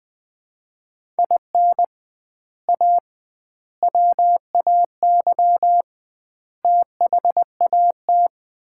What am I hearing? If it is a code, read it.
Morse code, heard as a keyed tone, about 20 wpm, 700 Hz. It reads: IN A WAY THAT
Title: 2x Word-Spacing - Sets of 4 Words - 20wpm